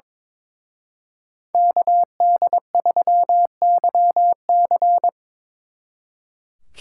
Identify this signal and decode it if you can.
Morse code, heard as a keyed tone, about 25 wpm, 700 Hz. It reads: KD3YC
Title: Speed Racing - USA 2x2 Call Signs 22wpm